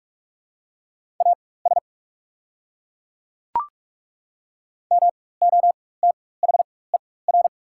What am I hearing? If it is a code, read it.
Morse code, heard as a keyed tone, about 42 wpm, spaced as for 21 wpm, 690 Hz. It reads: AS MOTHER
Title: 4x Character Spacing - Top 200 Words - 45wpm